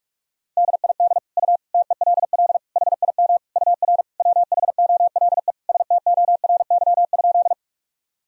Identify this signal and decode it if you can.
Morse code, heard as a keyed tone, about 47 wpm, 700 Hz. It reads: DID U TELL HIM UR WHOLE STORY?
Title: Sentences from Top 400 Words and Abbreviations 45wpm